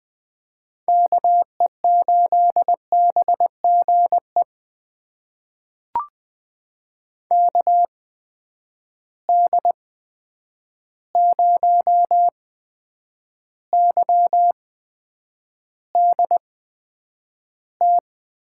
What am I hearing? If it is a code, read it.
Morse code, heard as a keyed tone, about 20 wpm, spaced as for 9 wpm, 700 Hz. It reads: KE8BGE KD0YDT
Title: USA Call Signs - 6 Characters - 8x Character Spacing - 20wpm